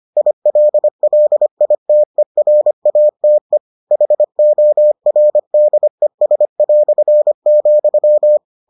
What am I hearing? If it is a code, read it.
Morse code, heard as a keyed tone, about 25 wpm, 600 Hz. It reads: ILLITERATE HORDES",